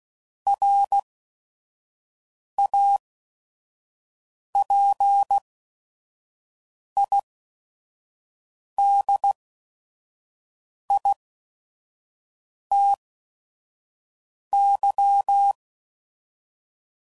Morse code, heard as a keyed tone, about 16 wpm, spaced as for 5 wpm, 800 Hz. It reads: RAPIDITY
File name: CW Code Groups @5WPM / Technical Subjects / Reactance